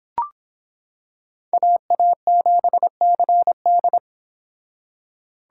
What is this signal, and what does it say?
Morse code, heard as a keyed tone, about 26 wpm, 700 Hz. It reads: AA7CD